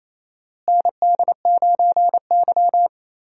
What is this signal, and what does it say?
Morse code, heard as a keyed tone, about 28 wpm, 700 Hz. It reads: ND9Y
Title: USA Call Signs - 4 Characters - Repeated 3x - 28wpm